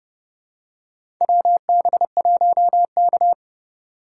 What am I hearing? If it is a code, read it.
Morse code, heard as a keyed tone, about 30 wpm, 700 Hz. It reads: WB1K
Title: USA Call Signs - 4 Characters - 3x Character Spacing - 30wpm